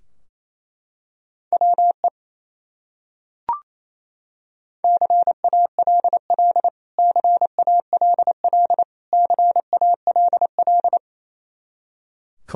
Morse code, heard as a keyed tone, about 28 wpm, 700 Hz. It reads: WE CALL CALL CALL
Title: Top 100 Words Repeated 28wpm